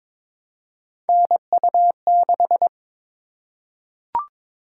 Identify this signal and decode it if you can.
Morse code, heard as a keyed tone, about 22 wpm, 700 Hz. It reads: NU6
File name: practice / 22wpm / Three Letter-Number Combinations - 22wpm